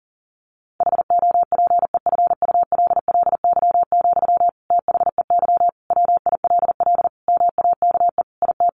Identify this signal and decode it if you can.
Morse code, heard as a keyed tone, about 40 wpm, 700 Hz. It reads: HOPEFULLY, THEY WILL MAKE IT
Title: Sentences from Top 1500 Words 40wpm